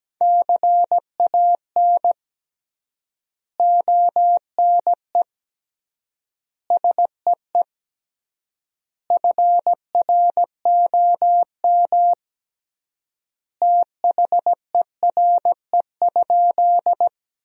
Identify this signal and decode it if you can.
Morse code, heard as a keyed tone, about 17 wpm, 700 Hz. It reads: CAN ONE SEE FROM THERE?
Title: 3x Word-Spacing - Sentences from Top 100 Words - 17wpm